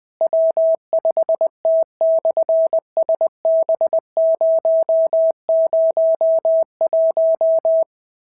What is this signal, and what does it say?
Morse code, heard as a keyed tone, about 20 wpm, 650 Hz. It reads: W5T/SB001
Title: SOTA QSO Practice 20wpm